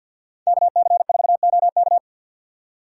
Morse code, heard as a keyed tone, about 50 wpm, 690 Hz. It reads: KC4OK
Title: USA Call Signs - 5 Characters - 6x Character Spacing - 50wpm